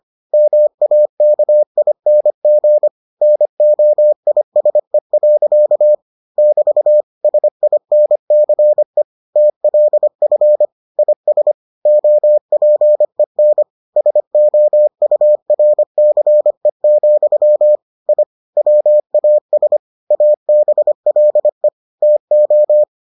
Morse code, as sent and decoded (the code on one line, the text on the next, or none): -- .- -.- .. -. --. / -. --- .. ... .-.-.- / -...- / ... .. -. -.-. / - .-.. ..-. / .. ... / --- .--. -. / ... --- ..- .-. -.-. --..-- / .. / .-- .- ... / .- -... .-.. / - ---
MAKING NOISE. <BT> SINCE TLF IS OPEN SOURCE, I WAS ABLE TO